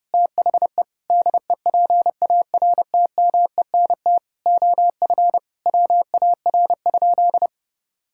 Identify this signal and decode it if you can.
Morse code, heard as a keyed tone, about 30 wpm, 700 Hz. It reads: THE DEPARTMENT OF WAR?
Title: Sentences from Top 1000 Words - Review Part 5 - 30wpm